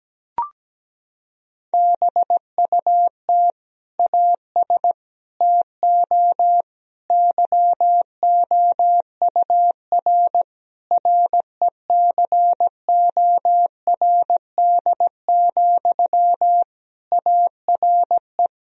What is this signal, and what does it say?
Morse code, heard as a keyed tone, about 17 wpm, 700 Hz. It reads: BUT AS TO YOUR RECORD, ARE
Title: Sentences from Top 1000 Words - Review Part 3 - 17wpm